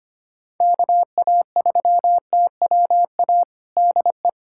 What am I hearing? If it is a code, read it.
Morse code, heard as a keyed tone, about 25 wpm, 700 Hz. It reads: KA3TWA DE
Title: USA Callsign DE Callsign - 6 Characters - 25wpm